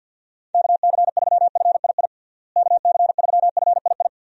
Morse code, heard as a keyed tone, about 52 wpm, 700 Hz. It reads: KC3FII KC3FII
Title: USA Call Signs - 6 Characters - Repeated 2x - 50wpm